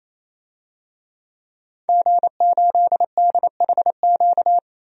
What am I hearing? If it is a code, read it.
Morse code, heard as a keyed tone, about 28 wpm, 700 Hz. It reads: G8DHQ
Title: DXCC-100 Call Signs - 5 Characters - 28wpm